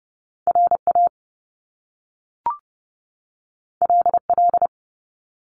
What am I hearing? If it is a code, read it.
Morse code, heard as a keyed tone, about 30 wpm, 700 Hz. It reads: R A L L
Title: Top Two Letter Combinations in Words 30wpm